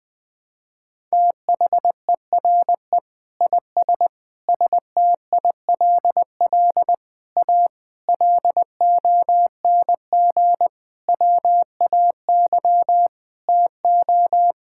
Morse code, heard as a keyed tone, about 20 wpm, 700 Hz. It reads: THERE IS STILL A LONG WAY TO
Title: Sentences with 8-Word Phrases - 20wpm